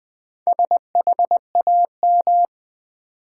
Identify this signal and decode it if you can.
Morse code, heard as a keyed tone, about 20 wpm, 700 Hz. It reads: SHAM
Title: Build A Word - 1 Letter At A Time - 5-Letter Words - 20wpm